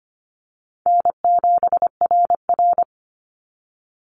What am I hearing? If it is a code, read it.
Morse code, heard as a keyed tone, about 25 wpm, 700 Hz. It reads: N7RR
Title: NA SOTA Activator Callsigns 25wpm